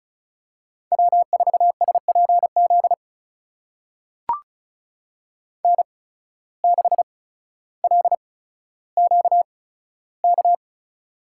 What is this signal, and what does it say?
Morse code, heard as a keyed tone, about 36 wpm, spaced as for 15 wpm, 700 Hz. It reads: W4SPZ N6LQK
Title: USA Call Signs - 5 Characters - 8x Character Spacing - 35wpm